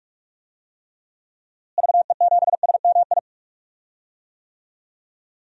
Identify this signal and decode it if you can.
Morse code, heard as a keyed tone, about 41 wpm, 700 Hz. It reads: VE7SMI
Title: DXCC-100 Call Signs - 6 Characters - Repeated 2x - 45wpm